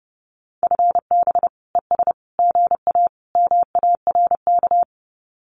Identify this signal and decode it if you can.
Morse code, heard as a keyed tone, about 29 wpm, 700 Hz. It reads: FB ES GA MARK
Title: Common QSO Phrases Encore 30wpm